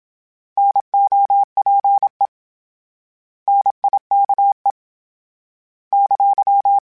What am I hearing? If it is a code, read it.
Morse code, heard as a keyed tone, about 27 wpm, 800 Hz. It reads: NOPE NIKE !